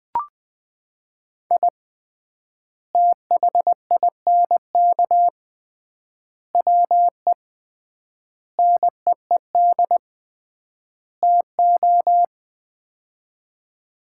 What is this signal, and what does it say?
Morse code, heard as a keyed tone, about 21 wpm, 700 Hz. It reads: I THINK WE NEED TO
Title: 3x Word-Spacing - Sets of 5 Words - 20wpm